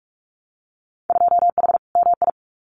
Morse code, heard as a keyed tone, about 46 wpm, 700 Hz. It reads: 2H MI
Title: Winter Field Day Exchanges Encore 45wpm